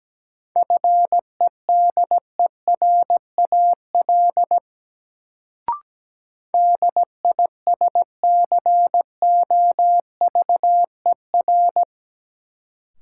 Morse code, heard as a keyed tone, about 17 wpm, 700 Hz. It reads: FEDERAL DISCOVER
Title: Sentences from Top 1100 Words - New Word Drill Alt - 17wpm